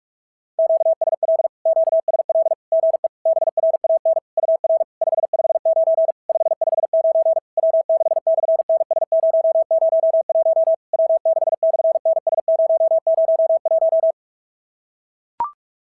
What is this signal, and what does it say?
Morse code, heard as a keyed tone, about 45 wpm, 660 Hz. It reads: QSL QSL GE BRAN UR 559 559 W6/NS001 W6/NS001
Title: SOTA QSO Practice 45wpm